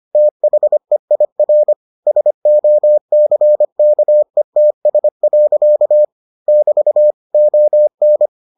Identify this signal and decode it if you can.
Morse code, heard as a keyed tone, about 25 wpm, 600 Hz. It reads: THEIR SOCKETS. <BT> ON